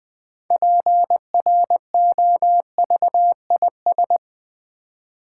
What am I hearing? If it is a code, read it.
Morse code, heard as a keyed tone, about 21 wpm, 700 Hz. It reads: PROVIS
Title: Build A Word - 3 Letters At A Time - 9-Letter Words - 20wpm